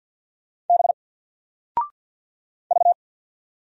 Morse code, heard as a keyed tone, about 50 wpm, 700 Hz. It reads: B V
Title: Single Character (Letter:Number:Punctuation:Prosign) Alt 50wpm